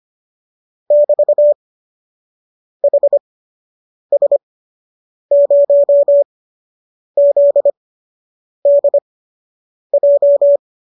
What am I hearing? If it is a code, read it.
Morse code, heard as a keyed tone, about 25 wpm, spaced as for 8 wpm, 590 Hz. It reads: <BT> HS0ZDJ